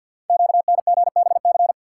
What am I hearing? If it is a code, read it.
Morse code, heard as a keyed tone, about 51 wpm, 700 Hz. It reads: QNGBC
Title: Code Groups - 5 Letters - 50wpm